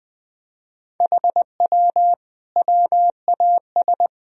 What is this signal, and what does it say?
Morse code, heard as a keyed tone, about 20 wpm, 700 Hz. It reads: HW WAS
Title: Sentences from Top 100 Words and Abbreviations 20wpm